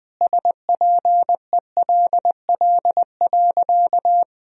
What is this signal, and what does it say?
Morse code, heard as a keyed tone, about 20 wpm, 700 Hz. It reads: SPELL.
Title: Sentences from Top 1000 Words - Review Part 1 - 20wpm